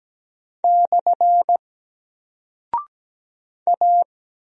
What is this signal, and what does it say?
Morse code, heard as a keyed tone, about 17 wpm, 700 Hz. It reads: / A
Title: Single Character (Letter:Number:Punctuation:Prosign) 17wpm